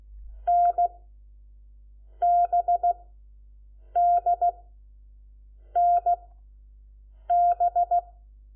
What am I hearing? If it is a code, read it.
Morse code, heard as a keyed tone, about 16 wpm, spaced as for 6 wpm, 700 Hz. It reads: NBDNB
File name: Lecciones / CLASE Nº4